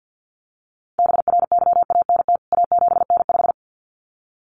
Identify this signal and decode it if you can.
Morse code, heard as a keyed tone, about 50 wpm, 700 Hz. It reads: BRYANT A7N5
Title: CWT Exchange 50wpm